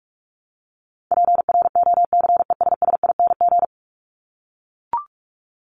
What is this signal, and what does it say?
Morse code, heard as a keyed tone, about 44 wpm, 700 Hz. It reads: PROCESSING
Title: ING Suffix - Top 100 Words with Base Words 45wpm